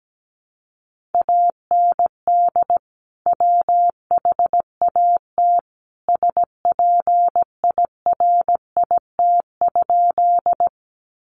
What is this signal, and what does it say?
Morse code, heard as a keyed tone, about 17 wpm, 700 Hz. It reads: AND WHAT SPIRIT?
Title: Sentences from Top 1400 Words 17wpm